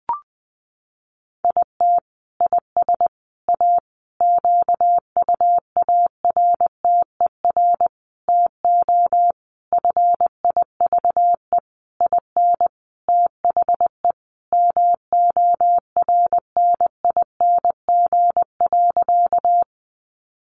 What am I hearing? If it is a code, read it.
Morse code, heard as a keyed tone, about 20 wpm, 700 Hz. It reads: IT IS A QUARTER TO FIVE IN THE MORNING.
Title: Sentences from Top 1300 Words 20wpm